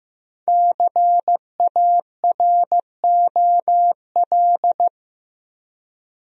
Morse code, heard as a keyed tone, about 15 wpm, 700 Hz. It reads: CAROL
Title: USA Common Names - 15wpm